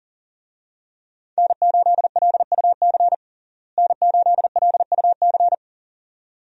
Morse code, heard as a keyed tone, about 40 wpm, 700 Hz. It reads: N8LUC N8LUC
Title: USA Call Signs - 5 Characters - Repeated 2x - 40wpm